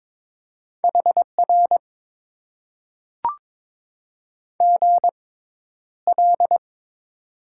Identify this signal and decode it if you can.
Morse code, heard as a keyed tone, about 22 wpm, spaced as for 11 wpm, 700 Hz. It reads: HR GL
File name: Top 90 QSO Elements - 6x Character Spacing - 22wpm